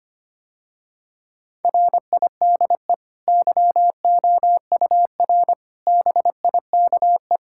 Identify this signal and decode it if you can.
Morse code, heard as a keyed tone, about 25 wpm, 690 Hz. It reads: RIDE YOUR BIKE
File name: Easy Sentences with Hints - 25wpm